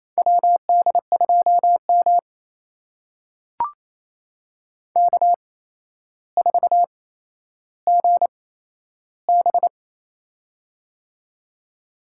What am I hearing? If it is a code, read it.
Morse code, heard as a keyed tone, about 28 wpm, spaced as for 12 wpm, 700 Hz. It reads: WD2M K4GB
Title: USA Call Signs - 4 Characters - 8x Character Spacing - 28wpm